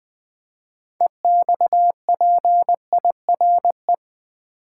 Morse code, heard as a keyed tone, about 20 wpm, 700 Hz. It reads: EXPIRE
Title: 6 Letter Words - First 3 Letters Given - 20wpm